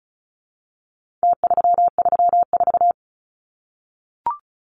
Morse code, heard as a keyed tone, about 34 wpm, 700 Hz. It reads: T334
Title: Numbers - Contest Style - 4-Digit 35wpm